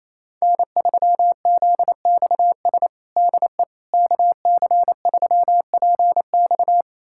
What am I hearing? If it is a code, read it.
Morse code, heard as a keyed tone, about 28 wpm, 700 Hz. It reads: N3ZXS DE KC3PX
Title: USA Callsign DE Callsign - 5 Characters - 28wpm